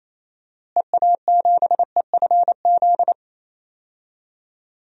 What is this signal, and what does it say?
Morse code, heard as a keyed tone, about 28 wpm, 700 Hz. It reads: EA7EFZ
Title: DXCC-100 Call Signs - 6 Characters - Repeated 2x - 28wpm